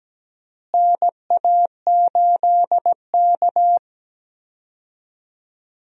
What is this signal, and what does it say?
Morse code, heard as a keyed tone, about 17 wpm, 700 Hz. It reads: NA8K
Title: USA Call Signs - 4 Characters - Repeated 3x - 17wpm